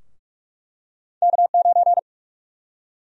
Morse code, heard as a keyed tone, about 41 wpm, 700 Hz. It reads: K9
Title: USA Call Sign Prefixes - 45wpm